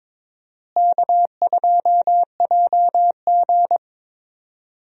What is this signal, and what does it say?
Morse code, heard as a keyed tone, about 22 wpm, 700 Hz. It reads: K2JG
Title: USA Call Signs - 4 Characters - Repeated 2x - 22wpm